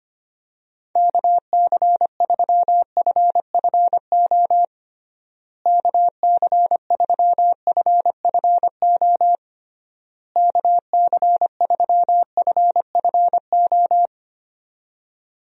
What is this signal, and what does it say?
Morse code, heard as a keyed tone, about 25 wpm, 700 Hz. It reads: KC3FFO KC3FFO KC3FFO
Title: USA Call Signs - 6 Characters - Repeated 3x - 25wpm